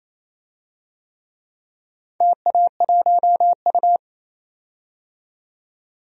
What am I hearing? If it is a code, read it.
Morse code, heard as a keyed tone, about 28 wpm, 700 Hz. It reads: TA1U
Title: DXCC-100 Call Signs - 4 Characters - 28wpm